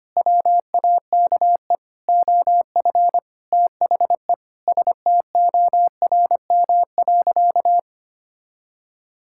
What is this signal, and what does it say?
Morse code, heard as a keyed tone, about 25 wpm, 700 Hz. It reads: WAKE OF THE STORM.
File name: practice / 25wpm / Sentences with 4-Word Phrases - 25wpm